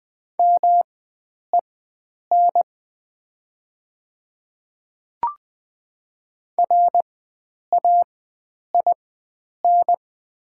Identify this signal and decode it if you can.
Morse code, heard as a keyed tone, about 20 wpm, spaced as for 9 wpm, 700 Hz. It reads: MEN RAIN
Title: 4x Character Spacing - Top 300 Words - 20wpm